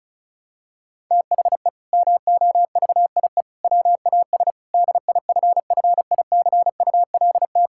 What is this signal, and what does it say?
Morse code, heard as a keyed tone, about 34 wpm, 700 Hz. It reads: THE MOVIE WAS DIFFICULT
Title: Sentences from Top 1000 Words - Review Part 5 - 35wpm